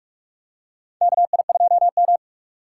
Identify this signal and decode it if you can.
Morse code, heard as a keyed tone, about 43 wpm, 700 Hz. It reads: KI2M
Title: USA Call Signs - 4 Characters - 3x Character Spacing - 45wpm